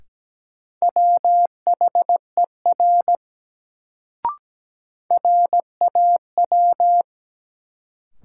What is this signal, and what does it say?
Morse code, heard as a keyed tone, about 17 wpm, 700 Hz. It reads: WHER RAW
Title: Sentences from Top 1500 Words - New Word Drill - 17wpm